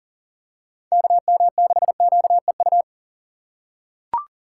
Morse code, heard as a keyed tone, about 40 wpm, 700 Hz. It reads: KM6QEU
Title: USA Call Signs - 6 Characters - 2x Character Spacing - 40wpm